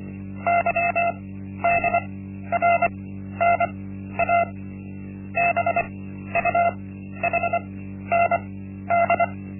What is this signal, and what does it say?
Morse code, heard as a keyed tone, about 24 wpm, spaced as for 12 wpm, 690 Hz. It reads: YDRNA BUHND
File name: Linksmakalnis-1